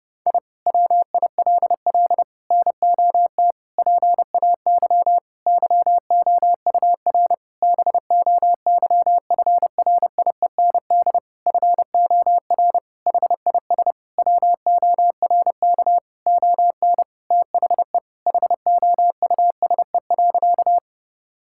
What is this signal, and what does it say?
Morse code, heard as a keyed tone, about 30 wpm, 700 Hz. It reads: I WILL NOT PAY YOUR BOYFRIEND FOR HIS WORK ON THE HOUSE.
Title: Sentences from Top 1000 Words - Review Part 5 - 30wpm